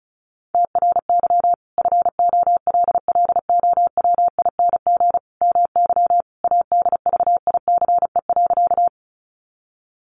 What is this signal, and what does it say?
Morse code, heard as a keyed tone, about 34 wpm, 700 Hz. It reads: TRY FOLLOWING MY ADVICE.
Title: Sentences from Top 1100 Words 35wpm